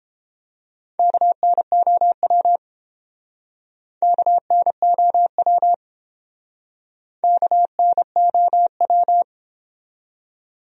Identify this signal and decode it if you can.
Morse code, heard as a keyed tone, about 29 wpm, 700 Hz. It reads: KNOW KNOW KNOW